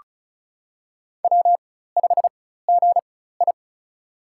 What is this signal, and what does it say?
Morse code, heard as a keyed tone, about 34 wpm, spaced as for 16 wpm, 700 Hz. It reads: W5GI
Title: USA Call Signs - 4 Characters - 4x Character Spacing - 35wpm